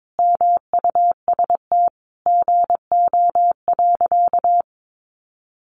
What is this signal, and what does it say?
Morse code, heard as a keyed tone, about 22 wpm, 700 Hz. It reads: MUST GO.
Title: Sentences from Top 1200 Words 22wpm